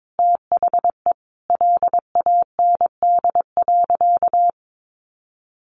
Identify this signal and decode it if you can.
Morse code, heard as a keyed tone, about 22 wpm, 700 Hz. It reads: THE LAND.
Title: Sentences from Top 1500 Words 22wpm